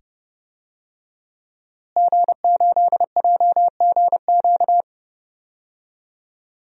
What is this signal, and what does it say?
Morse code, heard as a keyed tone, about 30 wpm, 700 Hz. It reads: G8JGQ
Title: DXCC-100 Call Signs - 5 Characters - Repeated 2x - 30wpm